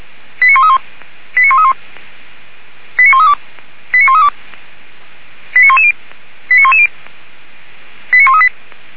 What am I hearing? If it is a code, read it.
Five-tone selective call (ZVEI), heard as quick runs of five tones, tones of about 70 ms. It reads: 89131 89131 89132 89132 89110 89110 89138
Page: 464/SDR/ZVEI/89133